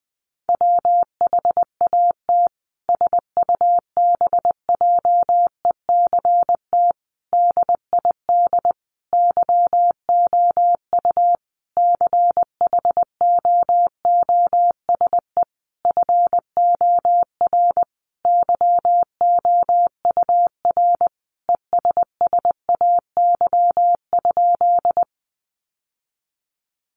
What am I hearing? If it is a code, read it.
Morse code, heard as a keyed tone, about 20 wpm, 700 Hz. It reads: WHAT SUBJECT DID YOU CHOOSE FOR YOUR ESSAY?
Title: Sentences from Top 1500 Words 20wpm